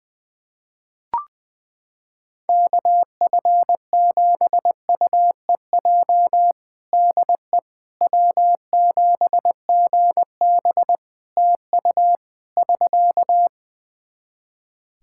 Morse code, heard as a keyed tone, about 20 wpm, 700 Hz. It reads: KF7UEJ DE W7GB TU <SK>
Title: Practice QSO - Exchange 4 - Ending - 20wpm